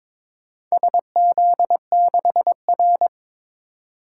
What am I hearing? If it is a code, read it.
Morse code, heard as a keyed tone, about 22 wpm, 700 Hz. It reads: SZ6R
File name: practice / 22wpm / Four Letter-Number Combinations - 22wpm